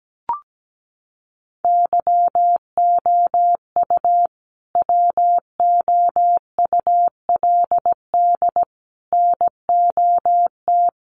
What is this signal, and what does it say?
Morse code, heard as a keyed tone, about 17 wpm, 700 Hz. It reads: YOU WOULD NOT
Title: Sets of 5 Words Encore 17wpm